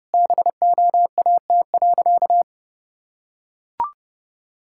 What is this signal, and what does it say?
Morse code, heard as a keyed tone, about 30 wpm, 700 Hz. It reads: BOAT.
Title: Sentences from Top 1000 Words - Review Part 6 - 30wpm